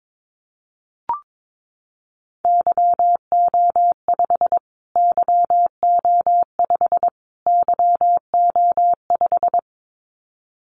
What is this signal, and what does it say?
Morse code, heard as a keyed tone, about 22 wpm, 700 Hz. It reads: YO5 YO5 YO5
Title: Callsign Prefixes for Top 25 DXCC - Repeated 3x - 22wpm